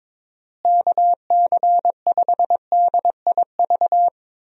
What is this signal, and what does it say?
Morse code, heard as a keyed tone, about 22 wpm, 700 Hz. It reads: KC5DIV
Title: USA Call Signs - 6 Characters - Repeated 2x - 22wpm